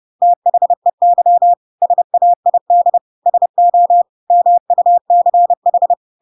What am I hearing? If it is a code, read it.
Morse code, heard as a keyed tone, about 30 wpm, 700 Hz. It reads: THEY SAID SO MUCH
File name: Pinocchio / Pinocchio ch3 30 30 ik1hge